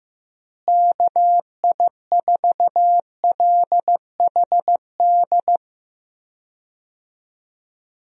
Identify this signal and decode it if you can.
Morse code, heard as a keyed tone, about 16 wpm, 700 Hz. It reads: KI4LHD
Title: USA Call Signs - 6 Characters - Repeated 2x - 15wpm